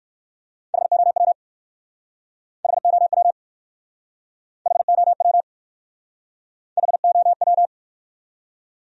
Morse code, heard as a keyed tone, about 58 wpm, 700 Hz. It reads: HOW HOW HOW HOW